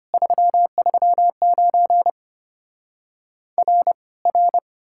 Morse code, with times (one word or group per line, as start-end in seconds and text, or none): 0.1-2.1 339
3.6-3.9 R
4.3-4.6 R